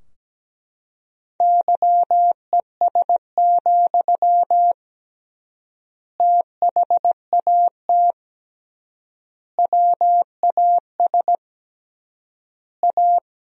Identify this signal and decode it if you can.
Morse code, heard as a keyed tone, about 17 wpm, 700 Hz. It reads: YES, THAT WAS A
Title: 3x Word-Spacing - Sentences from Top 500 Words - 17wpm